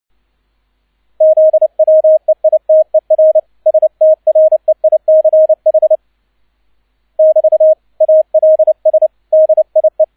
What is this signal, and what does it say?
Morse code, heard as a keyed tone, about 29 wpm, 630 Hz. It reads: ZWEITER STREICH <BT> ALS DIE
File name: 002 Zweiter Streich 30wpm